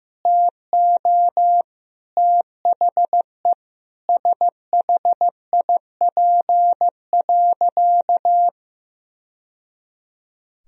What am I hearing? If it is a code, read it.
Morse code, heard as a keyed tone, about 15 wpm, 700 Hz. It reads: TO THE SHIP.
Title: Sentences from Top 1000 Words - Review Part 2 - 15wpm